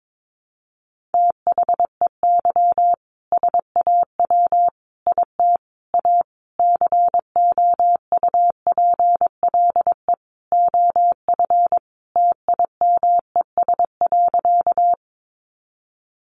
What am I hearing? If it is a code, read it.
Morse code, heard as a keyed tone, about 22 wpm, 700 Hz. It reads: THEY SAW IT A COUPLE OF TIMES.
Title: Sentences from Top 1100 Words 22wpm